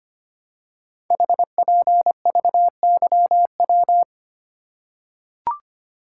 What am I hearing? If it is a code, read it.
Morse code, heard as a keyed tone, about 26 wpm, 700 Hz. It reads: HPVYW
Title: Code Groups - 5 Letters - 25wpm